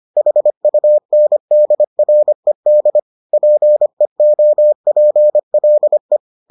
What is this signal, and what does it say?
Morse code, heard as a keyed tone, about 25 wpm, 600 Hz. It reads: HUNDRED PEOPLE